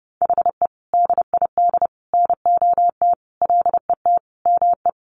Morse code, heard as a keyed tone, about 30 wpm, 700 Hz. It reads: HE DID NOT LET ME
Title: Sentences from Top 1200 Words 30wpm